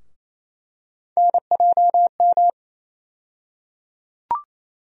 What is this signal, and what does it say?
Morse code, heard as a keyed tone, about 28 wpm, 700 Hz. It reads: NJM
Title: USA Call Sign Postfixes - 3 Letters 28wpm